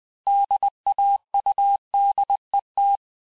Morse code, heard as a keyed tone, about 20 wpm, 800 Hz. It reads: DAUDET
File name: cours-du-mercredi / audios / cours-858-texte-clais-vitesse-19mpm